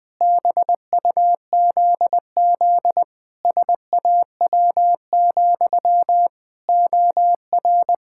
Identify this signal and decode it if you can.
Morse code, heard as a keyed tone, about 20 wpm, 700 Hz. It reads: BUZZ SAW, OR
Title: Pangram Challenge v2 - 20wpm